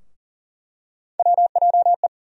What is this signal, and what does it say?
Morse code, heard as a keyed tone, about 40 wpm, 700 Hz. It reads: WJE
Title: USA Call Sign Postfixes - 3 Letters 40wpm